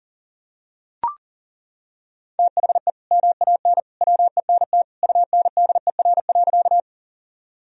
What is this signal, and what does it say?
Morse code, heard as a keyed tone, about 40 wpm, 700 Hz. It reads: THE MAN WENT UNDER.
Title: Sentences from Top 1000 Words - Review Part 1 - 40wpm